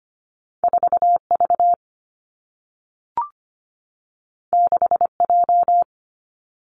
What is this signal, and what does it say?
Morse code, heard as a keyed tone, about 25 wpm, 700 Hz. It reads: 4 V 6 J
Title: Two Challenging Letter-Number Combinations 25wpm